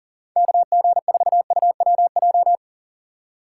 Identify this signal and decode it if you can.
Morse code, heard as a keyed tone, about 40 wpm, 700 Hz. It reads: KG4UWJ
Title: USA Call Signs - 6 Characters - 2x Character Spacing - 40wpm